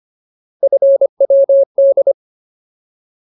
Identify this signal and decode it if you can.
Morse code, heard as a keyed tone, about 25 wpm, 550 Hz. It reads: FWD
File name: Ultimate Word List / 25WPM Ultimate